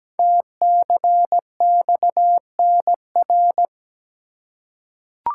Morse code, heard as a keyed tone, about 17 wpm, 700 Hz. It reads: TCXNR
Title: Code Groups - 5 Letters - 17wpm